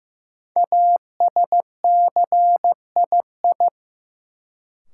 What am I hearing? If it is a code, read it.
Morse code, heard as a keyed tone, about 15 wpm, 700 Hz. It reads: ASCII
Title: Sets of 5 Words - New Word Drill - 15wpm